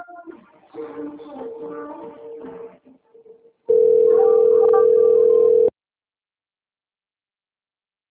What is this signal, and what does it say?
Ringback tone (North America), 440 + 480 Hz.